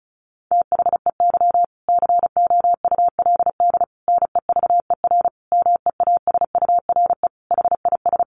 Morse code, heard as a keyed tone, about 34 wpm, 700 Hz. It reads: THEY COULD NEVER MEASURE HIS
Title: Sentences from Top 1200 Words 35wpm